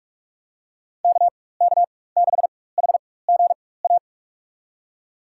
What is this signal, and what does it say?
Morse code, heard as a keyed tone, about 47 wpm, spaced as for 21 wpm, 700 Hz. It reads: KK6HGA
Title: USA Call Signs - 6 Characters - 4x Character Spacing - 45wpm